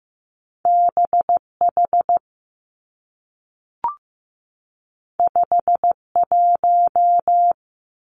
Morse code, heard as a keyed tone, about 15 wpm, 700 Hz. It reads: B H 5 1
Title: Two Challenging Letter-Number Combinations 15wpm